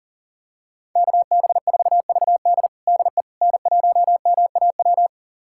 Morse code, heard as a keyed tone, about 40 wpm, 700 Hz. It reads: KB4VD DE N1MAW
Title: USA Callsign DE Callsign - 5 Characters - 40wpm